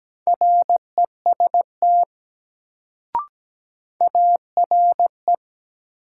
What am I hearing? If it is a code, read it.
Morse code, heard as a keyed tone, about 17 wpm, 700 Hz. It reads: REST ARE
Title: Sets of 2 Words - New Word Drill - 17wpm